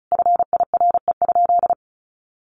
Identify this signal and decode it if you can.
Morse code, heard as a keyed tone, about 34 wpm, 700 Hz. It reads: FIRE?
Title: Sentences from Top 1000 Words 35wpm